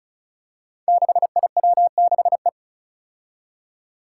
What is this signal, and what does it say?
Morse code, heard as a keyed tone, about 36 wpm, 700 Hz. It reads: 6IW6E
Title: Five Challenging Letter-Number Combinations 35wpm